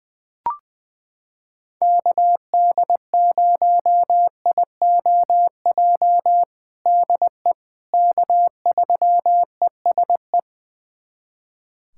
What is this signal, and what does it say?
Morse code, heard as a keyed tone, about 20 wpm, 700 Hz. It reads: KD0IOJ DE K3ESE
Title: USA Callsign DE Callsign 20wpm